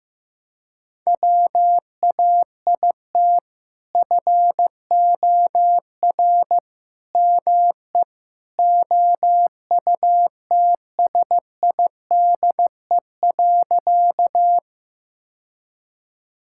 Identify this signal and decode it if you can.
Morse code, heard as a keyed tone, about 15 wpm, 700 Hz. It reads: WAIT FOR ME OUTSIDE.
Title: Sentences from Top 1000 Words - Review Part 6 - 15wpm